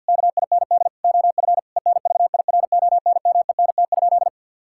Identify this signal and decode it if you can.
Morse code, heard as a keyed tone, about 50 wpm, 700 Hz. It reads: KIND OF ENVIRONMENT?
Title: Sentences from Top 1000 Words - Review Part 5 - 50wpm